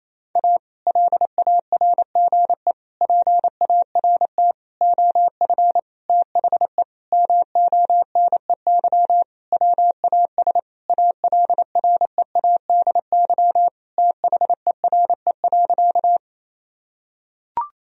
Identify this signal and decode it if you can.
Morse code, heard as a keyed tone, about 28 wpm, 700 Hz. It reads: A LARGE PART OF THE MONEY WAS ALREADY THERE.
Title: Sentences from Top 1000 Words - Review Part 1 - 28wpm